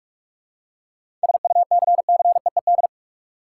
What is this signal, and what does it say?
Morse code, heard as a keyed tone, about 47 wpm, 700 Hz. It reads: SUCCEED
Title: Seven Letter Words - 45wpm